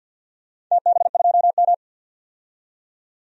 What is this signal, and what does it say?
Morse code, heard as a keyed tone, about 47 wpm, 700 Hz. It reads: TB2M
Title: DXCC-100 Call Signs - 4 Characters - 50wpm